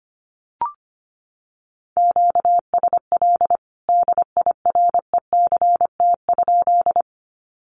Morse code, heard as a keyed tone, about 25 wpm, 700 Hz. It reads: QSL DIRECT?
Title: Common QSO Phrases 25wpm